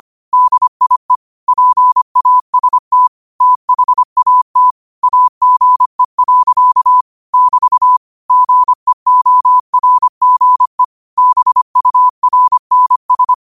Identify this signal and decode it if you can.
Morse code, heard as a keyed tone, about 25 wpm, 1000 Hz. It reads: DIE PAST THAT AGE. <BT> GEORGE BURNS
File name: qotdfiles / 25wpm